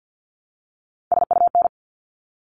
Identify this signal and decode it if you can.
Morse code, heard as a keyed tone, about 50 wpm, 700 Hz. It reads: SUN
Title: Top 500 Words 50wpm